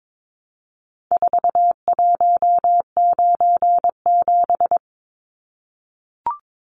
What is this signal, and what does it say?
Morse code, heard as a keyed tone, about 22 wpm, 700 Hz. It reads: 4197
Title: Numbers - 4-Digit 22wpm